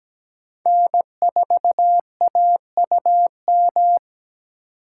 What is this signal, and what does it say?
Morse code, heard as a keyed tone, about 17 wpm, 700 Hz. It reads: N4AUM
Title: USA Call Signs - 4x Character Spacing - 17wpm